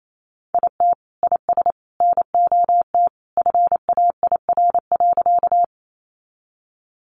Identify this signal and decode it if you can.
Morse code, heard as a keyed tone, about 28 wpm, 700 Hz. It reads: IT IS NOT FAIR.
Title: Sentences from Top 1300 Words 28wpm